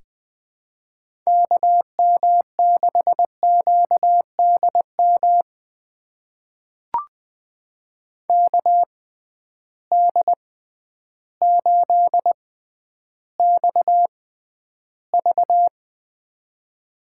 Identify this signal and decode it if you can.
Morse code, heard as a keyed tone, about 20 wpm, spaced as for 11 wpm, 700 Hz. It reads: KM6QDM KD8XV
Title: USA Call Signs - 6x Character Spacing - 20wpm